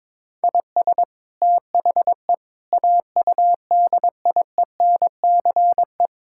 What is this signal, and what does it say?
Morse code, heard as a keyed tone, about 21 wpm, 700 Hz. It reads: IS THE AUDIENCE
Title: Sentences from Top 1000 Words - Review Part 5 - 22wpm